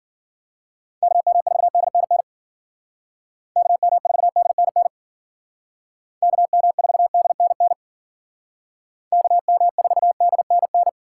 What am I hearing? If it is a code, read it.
Morse code, heard as a keyed tone, about 52 wpm, 700 Hz. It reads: KM4DNN KM4DNN KM4DNN KM4DNN